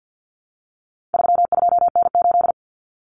Morse code, heard as a keyed tone, about 50 wpm, 700 Hz. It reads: 32N8
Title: Common Cut Numbers - 4-Digit 50wpm